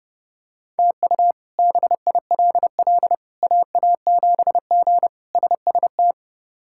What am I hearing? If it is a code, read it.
Morse code, heard as a keyed tone, about 30 wpm, 700 Hz. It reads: TU BILL AA7G SST
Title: K1USN SST QSO Practice - 30wpm